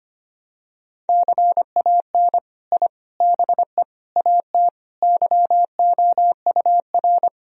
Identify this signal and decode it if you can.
Morse code, heard as a keyed tone, about 25 wpm, 700 Hz. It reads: CAN I BE AT YOUR
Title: Sentences from Top 1000 Words - Review Part 1 - 25wpm